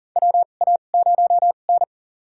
Morse code, heard as a keyed tone, about 40 wpm, spaced as for 29 wpm, 700 Hz. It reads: WA0N
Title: USA Call Signs - 4 Characters - 2x Character Spacing - 40wpm